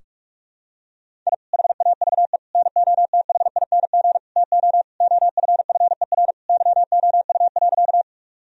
Morse code, heard as a keyed tone, about 46 wpm, 700 Hz. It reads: I HAVE NOTHING TO OFFER YOU.
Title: Sentences from Top 1000 Words - Review Part 4 - 45wpm